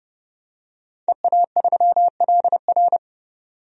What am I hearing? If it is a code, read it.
Morse code, heard as a keyed tone, about 30 wpm, 700 Hz. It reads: EA3LR
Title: DXCC-100 Call Signs - 5 Characters - 30wpm